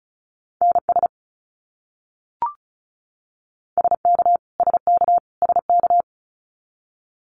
Morse code, heard as a keyed tone, about 34 wpm, 700 Hz. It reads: NS SK SK SK
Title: Canadian Provinces and Territories - Repeated 3x - 35wpm